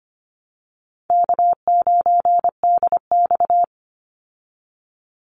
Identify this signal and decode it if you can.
Morse code, heard as a keyed tone, about 25 wpm, 700 Hz. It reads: K9DX
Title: Call Signs Repeated 3x - 25wpm